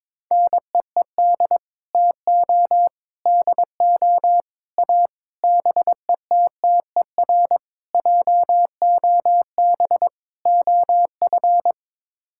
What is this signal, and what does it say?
Morse code, heard as a keyed tone, about 22 wpm, 700 Hz. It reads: NEED TO DO A BETTER JOB OF
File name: practice / 22wpm / Sets of 7 Words - 22wpm